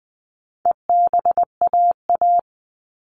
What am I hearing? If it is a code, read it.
Morse code, heard as a keyed tone, about 20 wpm, 700 Hz. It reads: EBAA
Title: All Cut Numbers - 4-Digit 20wpm